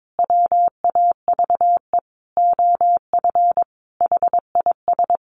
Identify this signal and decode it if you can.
Morse code, heard as a keyed tone, about 22 wpm, 700 Hz. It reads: WAVE OF HIS
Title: Sentences from Top 1400 Words 22wpm